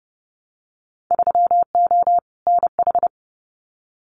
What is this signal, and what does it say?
Morse code, heard as a keyed tone, about 32 wpm, 700 Hz. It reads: 3O NH
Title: Winter Field Day Exchanges Encore 30wpm